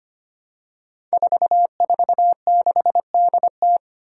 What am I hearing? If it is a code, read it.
Morse code, heard as a keyed tone, about 25 wpm, 700 Hz. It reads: 446DT
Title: Five Challenging Letter-Number Combinations 25wpm